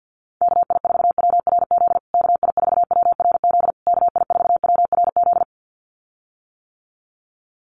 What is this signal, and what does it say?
Morse code, heard as a keyed tone, about 49 wpm, 700 Hz. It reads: KI4WRZ KI4WRZ KI4WRZ
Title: USA Call Signs Repeated 3x 50wpm